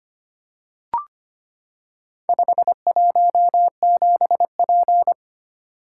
Morse code, heard as a keyed tone, about 25 wpm, 700 Hz. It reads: 517P
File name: practice / 25wpm / Four Letter-Number Combinations - 25wpm